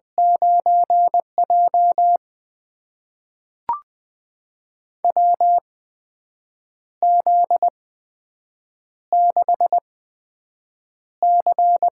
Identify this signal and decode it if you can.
Morse code, heard as a keyed tone, about 20 wpm, spaced as for 7 wpm, 700 Hz. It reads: 9J WZ6C
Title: USA Call Signs - 4 Characters - 8x Character Spacing - 20wpm